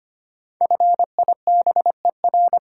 Morse code, heard as a keyed tone, about 25 wpm, 700 Hz. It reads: FIBER
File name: 5 Letter Words - First 3 Letters Given - 25wpm